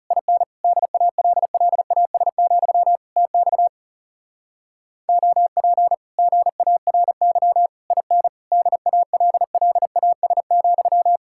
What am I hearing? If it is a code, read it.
Morse code, heard as a keyed tone, about 38 wpm, 700 Hz. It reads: IN DALLAS, TX OP GARY IN DALLAS,
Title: Speed-Racing Common QSO Phrases - 30wpm